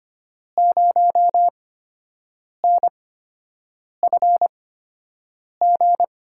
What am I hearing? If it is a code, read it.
Morse code, heard as a keyed tone, about 25 wpm, spaced as for 7 wpm, 700 Hz. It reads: 0NFG
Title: USA Call Signs - 5 Characters - 8x Character Spacing - 25wpm